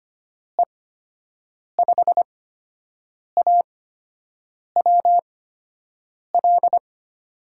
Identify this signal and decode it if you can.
Morse code, heard as a keyed tone, about 25 wpm, spaced as for 7 wpm, 700 Hz. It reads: E5AWL
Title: USA Call Signs - 6 Characters - 8x Character Spacing - 25wpm